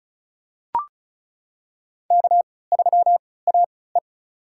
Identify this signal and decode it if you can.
Morse code, heard as a keyed tone, about 36 wpm, spaced as for 20 wpm, 700 Hz. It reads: K3AE
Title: USA Call Signs - 4 Characters - 3x Character Spacing - 35wpm